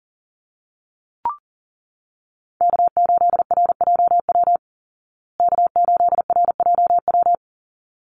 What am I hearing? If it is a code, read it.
Morse code, heard as a keyed tone, about 40 wpm, 700 Hz. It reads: K8RJW K8RJW
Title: Call Signs Repeated 3x - 40wpm